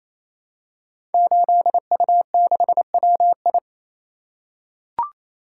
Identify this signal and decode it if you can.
Morse code, heard as a keyed tone, about 27 wpm, 700 Hz. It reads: 8U6WI
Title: Code Groups - 5 Letter:Numbers - 28wpm